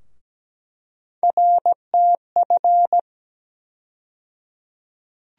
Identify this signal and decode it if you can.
Morse code, heard as a keyed tone, about 17 wpm, 700 Hz. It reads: RTF
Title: USA Call Sign Postfixes - 3 Letters 17wpm